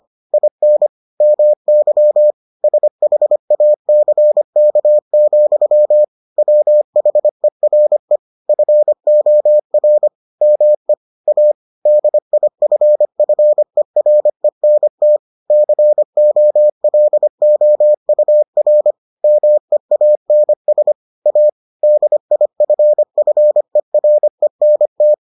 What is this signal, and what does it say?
Morse code, heard as a keyed tone, about 25 wpm, 600 Hz. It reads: IN MY SHACK, WHERE FOR ME A DIFFERENT COLOUR MEANS A DIFFERENT